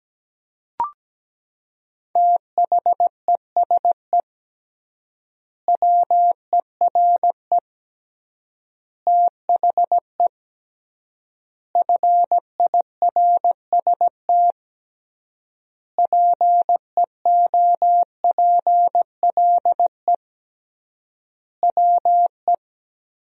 Morse code, heard as a keyed tone, about 17 wpm, 700 Hz. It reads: THESE WERE THE FIRST PEOPLE WE
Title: 3x Word-Spacing - Sentences from Top 300 Words - 17wpm